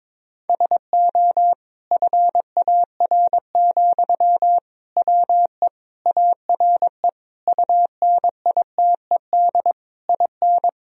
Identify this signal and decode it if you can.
Morse code, heard as a keyed tone, about 22 wpm, 700 Hz. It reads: SO FAR, WE ARE UNITED IN
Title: Sentences from Top 1000 Words - Review Part 4 - 22wpm